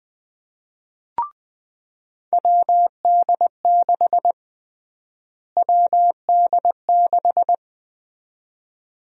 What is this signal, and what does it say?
Morse code, heard as a keyed tone, about 20 wpm, 700 Hz. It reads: WD6 WD6
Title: USA Call Sign Prefixes - Repeated 2x - 20wpm